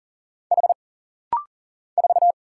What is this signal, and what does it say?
Morse code, heard as a keyed tone, about 41 wpm, 700 Hz. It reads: H 4
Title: Single Letter-Number - Rapid-Fire 40wpm